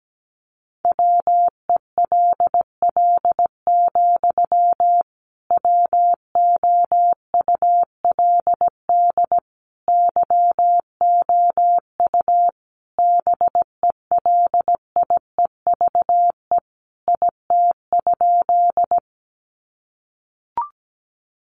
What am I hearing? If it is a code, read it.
Morse code, heard as a keyed tone, about 17 wpm, 700 Hz. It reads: WELL, WOULD YOU BELIEVE IT?
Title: Sentences from Top 1000 Words 17wpm